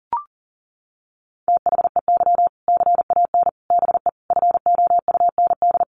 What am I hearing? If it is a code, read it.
Morse code, heard as a keyed tone, about 40 wpm, 700 Hz. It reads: THEY CAN BE FOUND
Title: Sentences from Top 1500 Words 40wpm